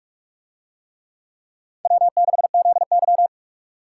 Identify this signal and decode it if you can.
Morse code, heard as a keyed tone, about 46 wpm, 700 Hz. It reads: W6ZY